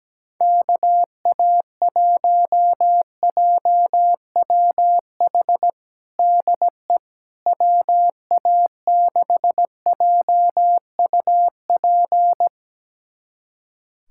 Morse code, heard as a keyed tone, about 17 wpm, 700 Hz. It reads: KA1JWH DE WA6JUP
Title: USA Callsign DE Callsign - 6 Characters - 17wpm